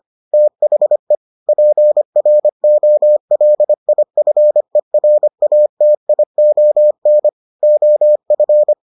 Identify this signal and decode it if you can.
Morse code, heard as a keyed tone, about 25 wpm, 600 Hz. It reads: THE PROLIFERATION OF